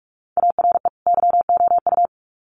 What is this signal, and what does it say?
Morse code, heard as a keyed tone, about 44 wpm, 700 Hz. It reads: ARE YOU
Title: Sets of 2 Words 45wpm